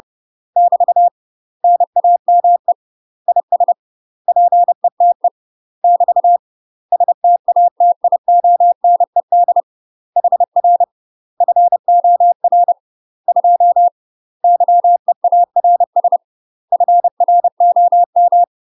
Morse code, heard as a keyed tone, about 30 wpm, 700 Hz. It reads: <BT> NAME IS PETE <BT> STATIONED HR FOR 2 YEARS FROM
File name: UK MASTER QSOs TRAINING / 30WPM / UK MASTER QSOs 0050